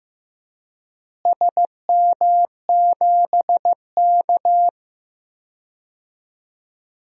Morse code, heard as a keyed tone, about 15 wpm, 700 Hz. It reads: SM7K
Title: DXCC-100 Call Signs - 4 Characters - Repeated 2x - 15wpm